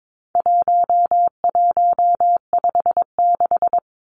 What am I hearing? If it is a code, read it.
Morse code, heard as a keyed tone, about 22 wpm, 700 Hz. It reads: 1156
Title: Numbers - 4-Digit 22wpm